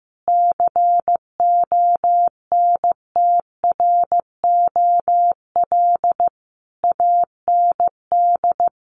Morse code, heard as a keyed tone, about 15 wpm, 700 Hz. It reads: CONTROL AND
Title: Sets of 5 Words Encore2 15wpm